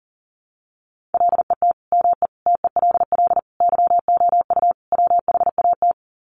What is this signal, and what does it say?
Morse code, heard as a keyed tone, about 40 wpm, 700 Hz. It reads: LET ME TELL YOU WHAT
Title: Sets of 5 Words Encore 40wpm